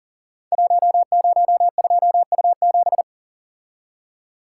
Morse code, heard as a keyed tone, about 40 wpm, 700 Hz. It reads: 102U7
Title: Code Groups - 5 Letter:Numbers - 40wpm